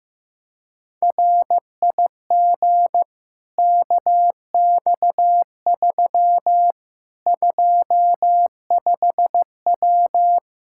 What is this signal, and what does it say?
Morse code, heard as a keyed tone, about 15 wpm, 700 Hz. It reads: RIG KX3 25W
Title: Practice QSO - Exchange 2 - Info - 15wpm